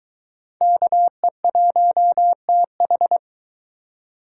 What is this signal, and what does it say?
Morse code, heard as a keyed tone, about 22 wpm, 700 Hz. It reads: KE1TH